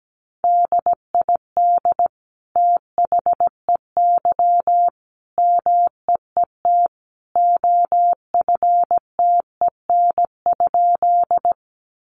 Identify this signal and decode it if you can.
Morse code, heard as a keyed tone, about 17 wpm, 700 Hz. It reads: DID THEY MEET OFTEN?
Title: Sentences from Top 1100 Words 17wpm